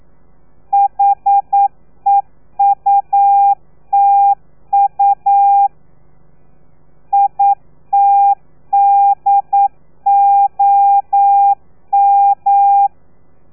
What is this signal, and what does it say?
Morse code, heard as a keyed tone, about 9 wpm, 800 Hz. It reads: HEUTU ITDOM
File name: telegrafia / i2tfj / Lez3 40